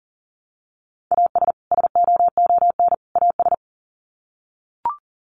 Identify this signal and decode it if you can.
Morse code, heard as a keyed tone, about 40 wpm, 700 Hz. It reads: AS SOON AS
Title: Sets of 3 Words 40wpm